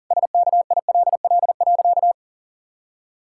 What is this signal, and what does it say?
Morse code, heard as a keyed tone, about 40 wpm, 700 Hz. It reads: SKILL.
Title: Pangram Challenge v2 - 40wpm